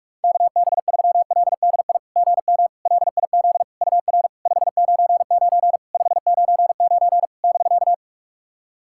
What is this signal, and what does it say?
Morse code, heard as a keyed tone, about 46 wpm, 700 Hz. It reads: KB3LDI GM LIZ UR 599 599 <BK>
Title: SOTA QSO Practice 45wpm